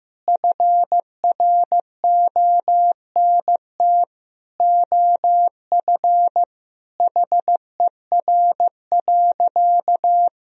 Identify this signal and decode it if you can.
Morse code, heard as a keyed tone, about 15 wpm, 700 Hz. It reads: FRONT OF HER.
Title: Sentences from Top 1000 Words - Review Part 2 - 15wpm